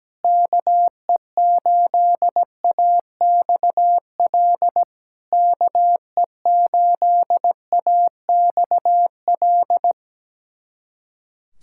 Morse code, heard as a keyed tone, about 17 wpm, 700 Hz. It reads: KE8AXL KE8AXL
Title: SOTA QSO Practice 17wpm